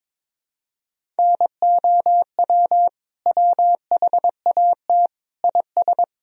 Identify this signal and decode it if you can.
Morse code, heard as a keyed tone, about 22 wpm, 700 Hz. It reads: NOW WHAT IS
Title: Sentences from Top 200 Words and Abbreviations 22wpm